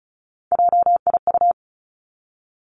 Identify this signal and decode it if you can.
Morse code, heard as a keyed tone, about 36 wpm, 700 Hz. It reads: JIU
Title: Three Challenging Letter-Number Combinations 35wpm